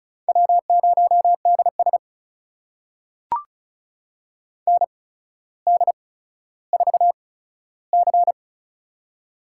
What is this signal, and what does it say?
Morse code, heard as a keyed tone, about 36 wpm, spaced as for 15 wpm, 700 Hz. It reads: W0DS ND4C
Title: USA Call Signs - 4 Characters - 8x Character Spacing - 35wpm